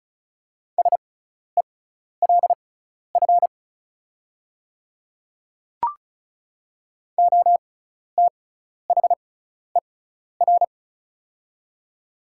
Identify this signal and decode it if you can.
Morse code, heard as a keyed tone, about 34 wpm, spaced as for 12 wpm, 690 Hz. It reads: SELF OTHER
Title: 6x Character Spacing - Top 200 Words - 35wpm